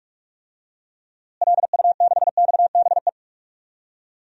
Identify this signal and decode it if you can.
Morse code, heard as a keyed tone, about 46 wpm, 700 Hz. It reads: LU6XBE
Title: DXCC-100 Call Signs - 6 Characters - Repeated 2x - 45wpm